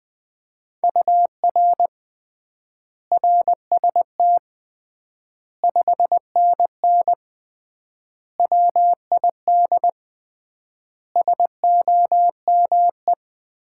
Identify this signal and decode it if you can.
Morse code, heard as a keyed tone, about 21 wpm, 700 Hz. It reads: UR RST 5NN WID SOME
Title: Common QSO Phrases - 3x Word-Spacing - 20wpm